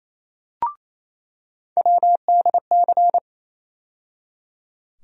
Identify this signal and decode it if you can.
Morse code, heard as a keyed tone, about 28 wpm, 700 Hz. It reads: WDC
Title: USA Call Sign Postfixes - 3 Letters 28wpm